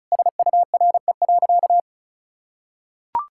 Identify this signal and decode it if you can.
Morse code, heard as a keyed tone, about 34 wpm, 700 Hz. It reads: SURE.
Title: Sentences from Top 1000 Words - Review Part 5 - 35wpm